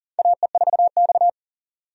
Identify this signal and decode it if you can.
Morse code, heard as a keyed tone, about 40 wpm, 700 Hz. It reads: AE4X
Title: USA Call Signs - 4 Characters - 40wpm